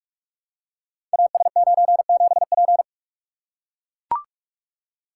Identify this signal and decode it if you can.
Morse code, heard as a keyed tone, about 44 wpm, 700 Hz. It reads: AS97P
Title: Code Groups - 5 Letter:Numbers - 45wpm